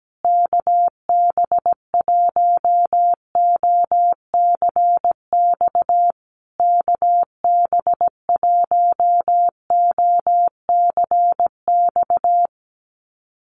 Morse code, heard as a keyed tone, about 17 wpm, 700 Hz. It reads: KB1OCX KB1OCX
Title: USA Call Signs Repeated 3x 17wpm